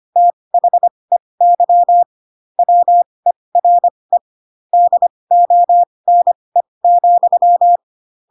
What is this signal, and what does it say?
Morse code, heard as a keyed tone, about 25 wpm, spaced as for 20 wpm, 700 Hz. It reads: THEY WERE DONE,
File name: Pinocchio / Pinocchio ch3 15 25 ik1hge